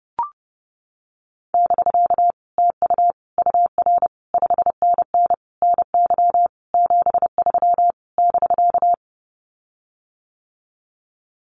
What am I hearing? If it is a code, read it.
Morse code, heard as a keyed tone, about 30 wpm, 700 Hz. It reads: <BK> TU UR 5NN NY 73 <BK>
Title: Common QSO Phrases Encore 30wpm